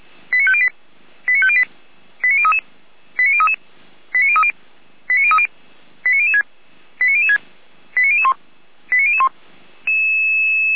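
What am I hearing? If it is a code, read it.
Five-tone selective call (ZVEI), heard as quick runs of five tones, tones of about 70 ms. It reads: 89508 89508 89033 89033 89030 89030 89006 89006 89001 89001
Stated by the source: ZVEI five-tone selective call received with an SDR.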